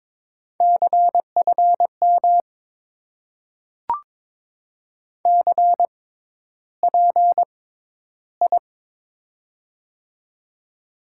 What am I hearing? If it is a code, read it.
Morse code, heard as a keyed tone, about 22 wpm, spaced as for 11 wpm, 700 Hz. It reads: CFM CPI
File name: Top 90 QSO Elements - 6x Character Spacing - 22wpm